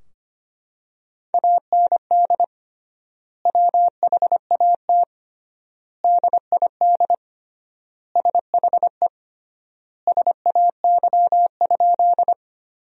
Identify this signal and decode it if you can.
Morse code, heard as a keyed tone, about 25 wpm, 700 Hz. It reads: AND WHAT DID SHE SAY?
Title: 3x Word-Spacing - Sentences from Top 200 Words - 25wpm